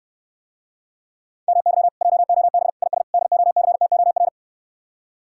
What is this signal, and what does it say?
Morse code, heard as a keyed tone, about 69 wpm, 700 Hz. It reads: MY JOB IS DOCTOR